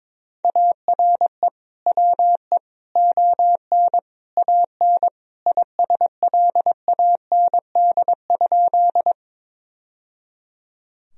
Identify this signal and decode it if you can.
Morse code, heard as a keyed tone, about 22 wpm, 700 Hz. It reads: ARE WE ON AN ISLAND?
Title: Sentences from Top 1000 Words - Review Part 3 - 22wpm